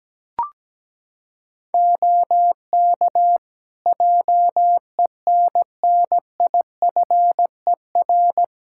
Morse code, heard as a keyed tone, about 17 wpm, 700 Hz. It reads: OK JENNIFER
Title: Practice QSO - Exchange 3 - More Info - 17wpm